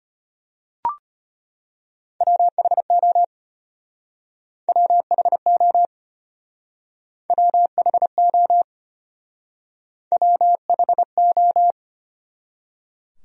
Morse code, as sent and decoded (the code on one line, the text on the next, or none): .-- .... --- / .-- .... --- / .-- .... --- / .-- .... ---
WHO WHO WHO WHO